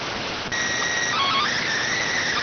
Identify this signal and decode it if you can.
SSTV (slow-scan television), PD 290.